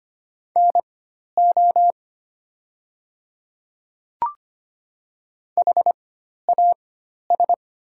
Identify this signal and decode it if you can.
Morse code, heard as a keyed tone, about 25 wpm, spaced as for 12 wpm, 700 Hz. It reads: NO HAS
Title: 4x Character Spacing - Top 100 Words - 25wpm